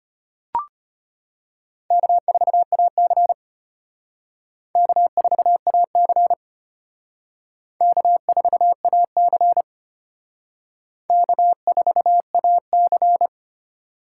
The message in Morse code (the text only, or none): K4AC K4AC K4AC K4AC